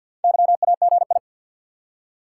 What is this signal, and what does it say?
Morse code, heard as a keyed tone, about 48 wpm, 700 Hz. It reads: YAGI
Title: Top 180 QSO Elements 50wpm